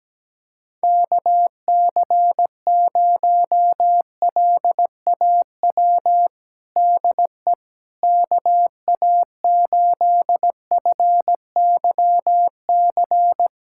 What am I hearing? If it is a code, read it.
Morse code, heard as a keyed tone, about 17 wpm, 700 Hz. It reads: KC0LAW DE KA8FYC
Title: USA Callsign DE Callsign - 6 Characters - 17wpm